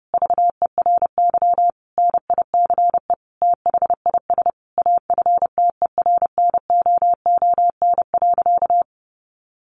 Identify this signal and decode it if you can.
Morse code, heard as a keyed tone, about 30 wpm, 700 Hz. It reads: VERY NICE THIS AFTERNOON.
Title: Sentences from Top 1500 Words 30wpm